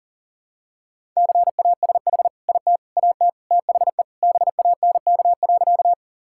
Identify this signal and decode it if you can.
Morse code, heard as a keyed tone, about 40 wpm, 700 Hz. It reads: CASH IT AT THE BANK.
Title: Sentences from Top 1000 Words - Review Part 5 - 40wpm